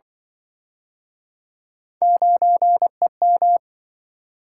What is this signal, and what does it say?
Morse code, heard as a keyed tone, about 28 wpm, 700 Hz. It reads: 9EM